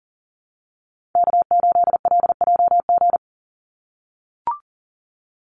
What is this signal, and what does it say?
Morse code, heard as a keyed tone, about 40 wpm, 700 Hz. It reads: K8LJG
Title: NA SOTA Chaser Callsigns 40wpm